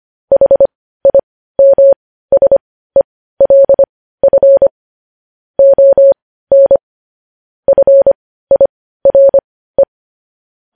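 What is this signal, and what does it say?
Morse code, heard as a keyed tone, about 25 wpm, spaced as for 15 wpm, 550 Hz. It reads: HIMSELF ON FIRE
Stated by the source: Morse code news headlines sent at 15.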